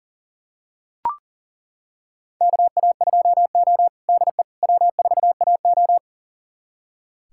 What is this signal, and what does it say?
Morse code, heard as a keyed tone, about 40 wpm, 700 Hz. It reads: KA2O DE W4AO
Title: USA Callsign DE Callsign - 4 Characters - 40wpm